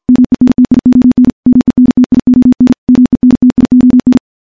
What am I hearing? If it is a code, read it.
Morse code, heard as a keyed tone, about 46 wpm, 250 Hz. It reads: MENTION MENTION MENTION
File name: morse / morse-training-50WPM